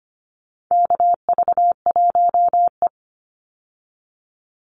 Morse code, heard as a keyed tone, about 25 wpm, 700 Hz. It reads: KV1E
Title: Call Signs Repeated 2x - 25wpm